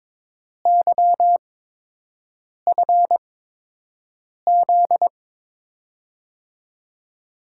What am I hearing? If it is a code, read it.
Morse code, heard as a keyed tone, about 22 wpm, spaced as for 6 wpm, 700 Hz. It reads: YFZ
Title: USA Call Signs - 8x Character Spacing - 22wpm